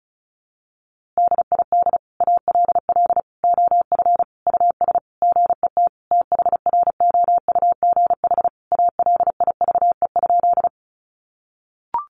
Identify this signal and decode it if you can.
Morse code, heard as a keyed tone, about 34 wpm, 700 Hz. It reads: DID ALL OF US GET THROUGH ALIVE?
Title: A Sound Of Thunder 35wpm